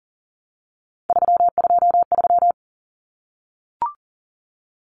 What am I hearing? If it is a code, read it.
Morse code, heard as a keyed tone, about 40 wpm, 700 Hz. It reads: 323
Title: Common Cut Numbers - 3-Digit 40wpm